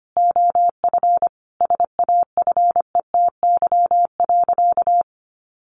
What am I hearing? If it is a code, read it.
Morse code, heard as a keyed tone, about 25 wpm, 700 Hz. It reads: OF SAFETY.
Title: Sentences from Top 1000 Words 25wpm